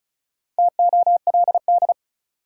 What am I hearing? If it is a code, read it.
Morse code, heard as a keyed tone, about 36 wpm, 700 Hz. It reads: TOLD
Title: Sentences from Top 400 Words - New Word Drill - 35wpm